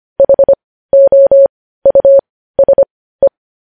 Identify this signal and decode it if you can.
Morse code, heard as a keyed tone, about 25 wpm, spaced as for 15 wpm, 550 Hz. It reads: HOUSE
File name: podcast / News headlines-1x-25x15